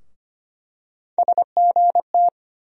Morse code, heard as a keyed tone, about 25 wpm, 700 Hz. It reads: SGT
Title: USA Call Sign Postfixes - 3 Letters 25wpm